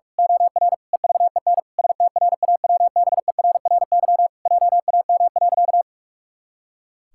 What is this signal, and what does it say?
Morse code, heard as a keyed tone, about 43 wpm, 700 Hz. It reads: OR EVEN STRAWBERRY JAM.
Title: Pangram Challenge v2 - 45wpm